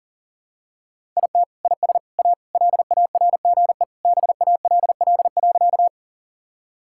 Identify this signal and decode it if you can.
Morse code, heard as a keyed tone, about 40 wpm, 700 Hz. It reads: IT IS A LARGE BALL.
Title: Sentences from Top 1000 Words - Review Part 6 - 40wpm